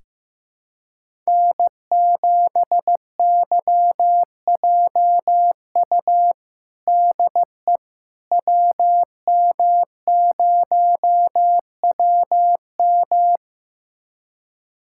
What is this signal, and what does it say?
Morse code, heard as a keyed tone, about 16 wpm, 700 Hz. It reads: N7YJU DE WM0WM
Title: USA Callsign DE Callsign - 5 Characters - 15wpm